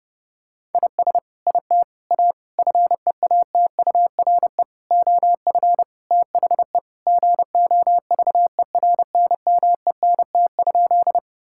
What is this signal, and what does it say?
Morse code, heard as a keyed tone, about 30 wpm, 700 Hz. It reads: IS IT A FEATURE OF THE GOVERNMENT?
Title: Sentences from Top 1000 Words - Review Part 5 - 30wpm